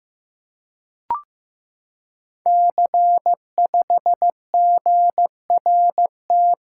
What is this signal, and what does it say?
Morse code, heard as a keyed tone, about 15 wpm, 700 Hz. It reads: C5GRT
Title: Code Groups - 5 Letter:Numbers - 15wpm